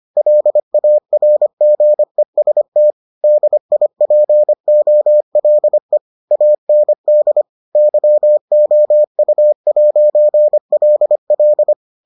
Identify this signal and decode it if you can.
Morse code, heard as a keyed tone, about 25 wpm, 600 Hz. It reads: LARGEST DIPOLE AND YOU'LL